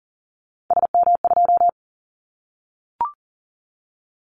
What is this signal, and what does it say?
Morse code, heard as a keyed tone, about 40 wpm, 700 Hz. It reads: SM2
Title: Callsign Prefixes for Top 25 DXCC - Repeated 3x - 40wpm